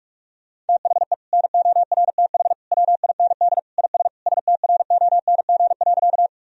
Morse code, heard as a keyed tone, about 43 wpm, 700 Hz. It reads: THE NORTH WIND IS STRONG.
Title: Sentences from Top 1000 Words - Review Part 2 - 45wpm